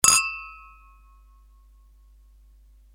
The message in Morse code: none